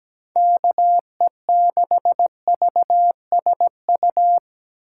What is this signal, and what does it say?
Morse code, heard as a keyed tone, about 17 wpm, 700 Hz. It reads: KE6VSU
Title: USA Call Signs - 6 Characters - 2x Character Spacing - 17wpm